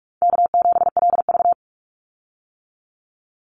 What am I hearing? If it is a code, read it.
Morse code, heard as a keyed tone, about 44 wpm, 700 Hz. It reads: K7LV
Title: Easy Call Signs - 45wpm